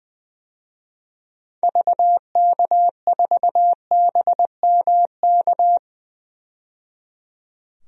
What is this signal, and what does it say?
Morse code, heard as a keyed tone, about 20 wpm, 700 Hz. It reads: VK4BMK
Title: DXCC-100 Call Signs - 6 Characters - 20wpm